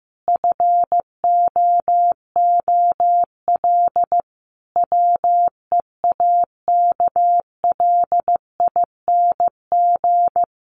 Morse code, heard as a keyed tone, about 15 wpm, 700 Hz. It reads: FOOL WEAKLING
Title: A Sound Of Thunder 15wpm